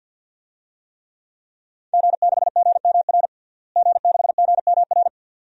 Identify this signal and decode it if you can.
Morse code, heard as a keyed tone, about 53 wpm, 700 Hz. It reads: G6GMR G6GMR
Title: DXCC-100 Call Signs - 5 Characters - Repeated 2x - 50wpm